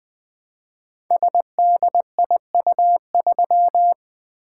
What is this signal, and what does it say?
Morse code, heard as a keyed tone, about 20 wpm, 700 Hz. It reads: SDIU3
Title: Code Groups - 5 Letter:Numbers - 20wpm